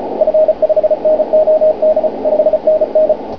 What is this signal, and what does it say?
Morse code, heard as a keyed tone, about 33 wpm, 620 Hz, with radio noise. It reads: F5NOD 5NN